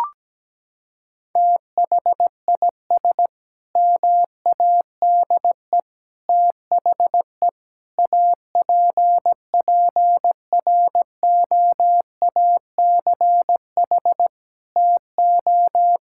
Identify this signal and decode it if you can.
Morse code, heard as a keyed tone, about 17 wpm, 700 Hz. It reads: THIS MADE THE APPROACH TO
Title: Sentences from Top 1000 Words - Review Part 6 - 17wpm